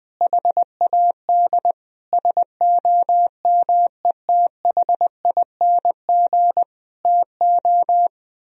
Morse code, heard as a keyed tone, about 21 wpm, 700 Hz. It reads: HAD SOMETHING TO
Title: Sentences from Top 1000 Words - Review Part 2 - 20wpm